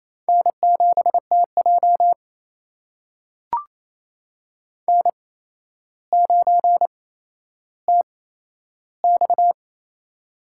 Morse code, heard as a keyed tone, about 28 wpm, spaced as for 12 wpm, 700 Hz. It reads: N7TJ N9TX
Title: USA Call Signs - 4 Characters - 8x Character Spacing - 28wpm